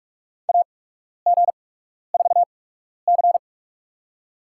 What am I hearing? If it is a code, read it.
Morse code, heard as a keyed tone, about 46 wpm, spaced as for 12 wpm, 700 Hz. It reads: AG4C
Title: USA Call Signs - 4 Characters - 8x Character Spacing - 45wpm